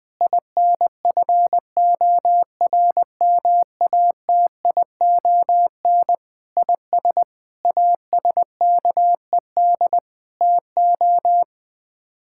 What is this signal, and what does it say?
Morse code, heard as a keyed tone, about 20 wpm, 700 Hz. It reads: INFORMATION IS ASKED TO
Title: Sets of 6 Words - 20wpm